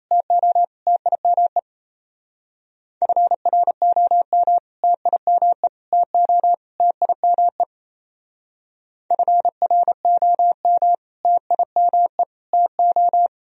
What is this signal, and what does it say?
Morse code, heard as a keyed tone, about 37 wpm, 700 Hz. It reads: TO TIME FROM TIME TO TIME FROM TIME TO